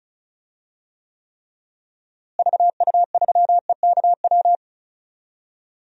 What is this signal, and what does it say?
Morse code, heard as a keyed tone, about 34 wpm, 700 Hz. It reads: VU3EKW
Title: DXCC-100 Call Signs - 6 Characters - Repeated 3x - 35wpm